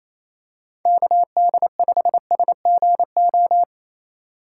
Morse code, heard as a keyed tone, about 28 wpm, 700 Hz. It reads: KD5SGO
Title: USA Call Signs - 6 Characters - 4x Character Spacing - 28wpm